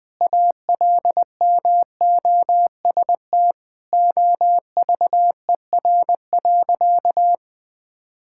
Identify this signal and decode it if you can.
Morse code, heard as a keyed tone, about 20 wpm, 700 Hz. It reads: ALMOST OVER.
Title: Sentences from Top 1600 Words 20wpm